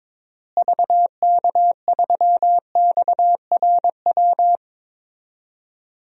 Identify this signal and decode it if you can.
Morse code, heard as a keyed tone, about 22 wpm, 700 Hz. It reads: VK3XRW
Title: DXCC-100 Call Signs - 6 Characters - Repeated 2x - 22wpm